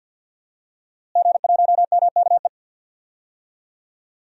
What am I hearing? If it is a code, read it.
Morse code, heard as a keyed tone, about 50 wpm, 700 Hz. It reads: G1MKE
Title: DXCC-100 Call Signs - 5 Characters - Repeated 3x - 50wpm